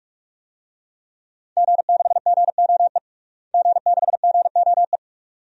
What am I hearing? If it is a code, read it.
Morse code, heard as a keyed tone, about 43 wpm, 700 Hz. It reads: G6GOE G6GOE
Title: DXCC-100 Call Signs - 5 Characters - Repeated 2x - 45wpm